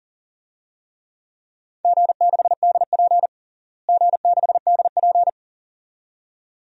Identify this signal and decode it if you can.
Morse code, heard as a keyed tone, about 37 wpm, 700 Hz. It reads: G6DP G6DP
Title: DXCC-100 Call Signs - 4 Characters - Repeated 2x - 40wpm